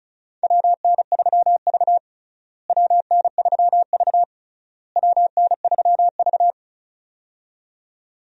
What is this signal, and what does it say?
Morse code, heard as a keyed tone, about 36 wpm, 700 Hz. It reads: WN3V WN3V WN3V
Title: USA Call Signs - 4 Characters - Repeated 3x - 35wpm